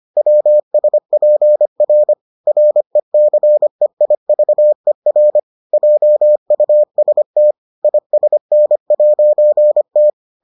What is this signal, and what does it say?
Morse code, heard as a keyed tone, about 25 wpm, 600 Hz. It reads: WSPR RECEIVER JUST ISN'T